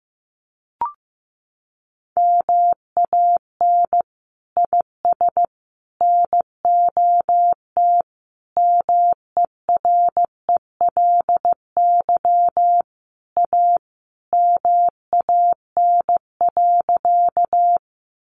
Morse code, heard as a keyed tone, about 15 wpm, 700 Hz. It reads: MAN IS NOT MERELY A MAN.
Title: Sentences from Top 1400 Words 15wpm